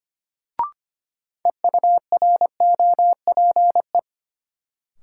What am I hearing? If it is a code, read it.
Morse code, heard as a keyed tone, about 25 wpm, 700 Hz. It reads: EUROPE
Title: Sets of 5 Words Encore - New Word Drill - 25wpm